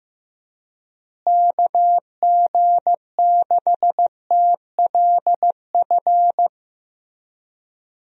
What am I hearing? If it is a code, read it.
Morse code, heard as a keyed tone, about 15 wpm, 700 Hz. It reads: KG6TLF
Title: USA Call Signs - 6 Characters - Repeated 3x - 15wpm